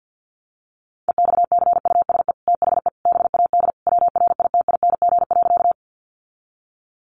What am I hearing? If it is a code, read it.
Morse code, heard as a keyed tone, about 50 wpm, 700 Hz. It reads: EXCUSE THE BAD WRITING.
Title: Sentences from Top 800 Words 50wpm